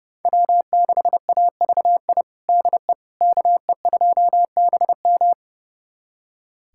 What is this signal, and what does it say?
Morse code, heard as a keyed tone, about 30 wpm, 700 Hz. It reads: W6AVI DE KE2BM
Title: USA Callsign DE Callsign - 5 Characters - 30wpm